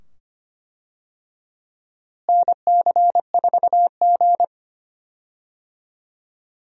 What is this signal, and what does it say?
Morse code, heard as a keyed tone, about 25 wpm, 700 Hz. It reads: NC4G
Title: USA Call Signs - 4 Characters - Repeated 2x - 25wpm